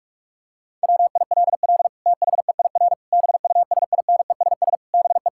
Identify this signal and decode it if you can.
Morse code, heard as a keyed tone, about 46 wpm, 700 Hz. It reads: WILL THEIR BUSINESS BE
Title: Sentences from Top 1000 Words - Review Part 5 - 45wpm